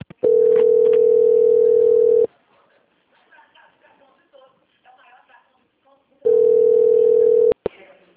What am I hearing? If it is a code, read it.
Ringback tone (North America), 440 + 480 Hz, with speech.
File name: out- - - - - .6860